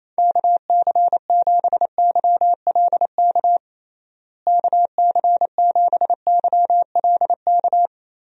Morse code, heard as a keyed tone, about 27 wpm, 700 Hz. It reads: KC7YLK KC7YLK
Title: USA Call Signs - 6 Characters - Repeated 2x - 28wpm